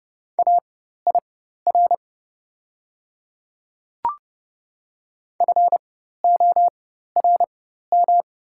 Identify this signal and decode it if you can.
Morse code, heard as a keyed tone, about 32 wpm, spaced as for 14 wpm, 700 Hz. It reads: AIR FORM
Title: 4x Character Spacing - Top 500 Words - 30wpm